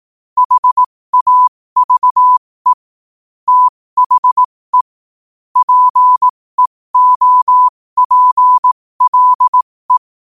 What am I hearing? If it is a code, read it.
Morse code, heard as a keyed tone, about 18 wpm, spaced as for 15 wpm, 1000 Hz. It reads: HAVE THE PEOPLE
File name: qotdfiles / 15wpm